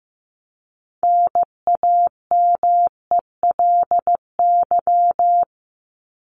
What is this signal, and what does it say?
Morse code, heard as a keyed tone, about 15 wpm, 700 Hz. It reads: NAMELY
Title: LY Suffix - Top 100 Words with Base Words 15wpm